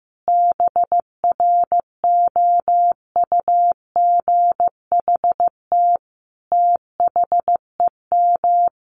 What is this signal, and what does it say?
Morse code, heard as a keyed tone, about 15 wpm, 700 Hz. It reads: BROUGHT THEM
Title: Sentences from Top 500 Words 15wpm